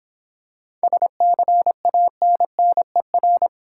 Morse code, heard as a keyed tone, about 26 wpm, 690 Hz. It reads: SCANNER